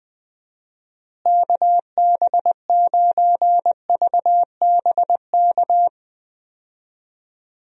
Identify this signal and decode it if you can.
Morse code, heard as a keyed tone, about 21 wpm, 700 Hz. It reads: KB9VBK
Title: USA Call Signs - 6 Characters - Repeated 2x - 20wpm